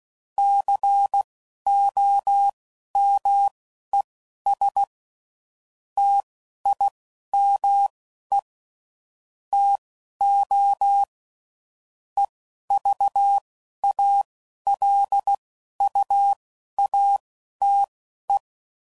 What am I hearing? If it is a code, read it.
Morse code, heard as a keyed tone, about 16 wpm, spaced as for 11 wpm, 800 Hz. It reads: COMES TIME TO EVALUATE
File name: CW Code Groups @10WPM / Technical Subjects / AND Gate